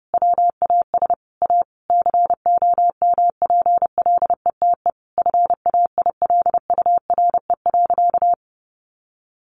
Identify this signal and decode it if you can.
Morse code, heard as a keyed tone, about 30 wpm, 700 Hz. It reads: WAS A COMPLETE FAILURE.
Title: Sentences from Top 1400 Words 30wpm